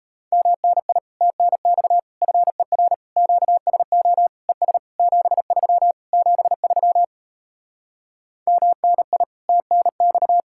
Morse code, heard as a keyed tone, about 37 wpm, 700 Hz. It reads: MNI TNX FER QSO ES 73 73 MNI TNX